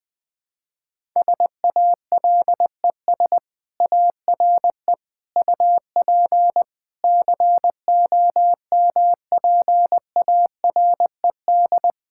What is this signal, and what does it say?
Morse code, heard as a keyed tone, about 21 wpm, 700 Hz. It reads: SALES ARE UP COMPARED
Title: Sentences with 7-Word Phrases - Spoken Beginning - 20wpm